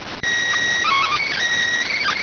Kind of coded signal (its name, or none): SSTV (slow-scan television)